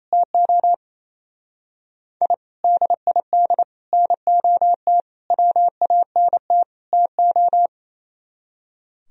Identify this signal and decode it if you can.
Morse code, heard as a keyed tone, about 34 wpm, 700 Hz. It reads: TO I DID NOT WANT TO
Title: Speed-Racing - Sets of 5 Words 28wpm